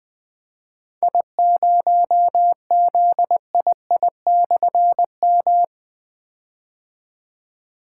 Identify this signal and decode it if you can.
Morse code, heard as a keyed tone, about 20 wpm, 700 Hz. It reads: I0ZII/M
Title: Difficult Call Signs 20wpm